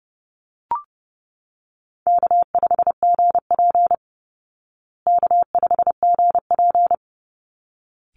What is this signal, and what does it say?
Morse code, heard as a keyed tone, about 30 wpm, 700 Hz. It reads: K5GP K5GP
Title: Call Signs Repeated 3x - 30wpm